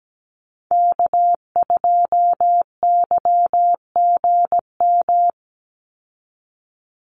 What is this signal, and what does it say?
Morse code, heard as a keyed tone, about 17 wpm, 700 Hz. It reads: K2YGM
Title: Call Signs Repeated 3x - 17wpm